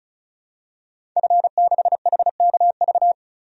Morse code, heard as a keyed tone, about 34 wpm, 700 Hz. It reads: F6HKV
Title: DXCC-100 Call Signs - 5 Characters - 35wpm